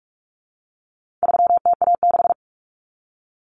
Morse code, heard as a keyed tone, about 44 wpm, 700 Hz. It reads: 3TA6
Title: Common Cut Numbers - 4-Digit 45wpm